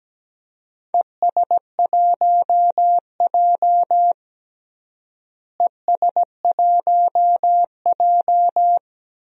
Morse code, heard as a keyed tone, about 17 wpm, 700 Hz. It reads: ES1J ES1J
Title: DXCC-100 Call Signs - 4 Characters - Repeated 2x - 17wpm